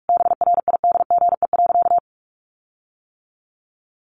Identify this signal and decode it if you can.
Morse code, heard as a keyed tone, about 44 wpm, 700 Hz. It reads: BRIDGE.
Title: Sentences from Top 1400 Words 45wpm